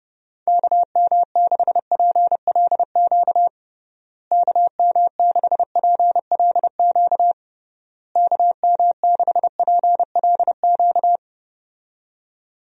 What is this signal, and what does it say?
Morse code, heard as a keyed tone, about 32 wpm, 700 Hz. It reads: KM6PLQ KM6PLQ KM6PLQ
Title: USA Call Signs - 6 Characters - Repeated 3x - 30wpm